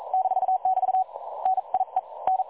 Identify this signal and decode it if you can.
Morse code, heard as a keyed tone, about 38 wpm, 750 Hz. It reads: -<BT> NIE N
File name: dlazaawansowanych / nagraniazpasma / websdr recording 3565.0kHz